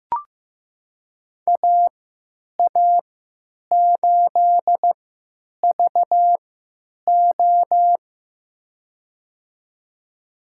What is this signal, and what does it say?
Morse code, heard as a keyed tone, about 15 wpm, spaced as for 9 wpm, 700 Hz. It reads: AA8VO
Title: USA Call Signs - 5 Characters - 3x Character Spacing - 15wpm